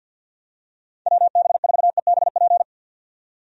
Morse code, heard as a keyed tone, about 48 wpm, 700 Hz. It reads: WB4EBP
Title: USA Call Signs - 6 Characters - 50wpm